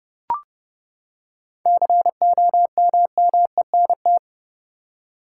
Morse code, heard as a keyed tone, about 31 wpm, 700 Hz. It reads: COMMENT
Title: Seven Letter Words - 30wpm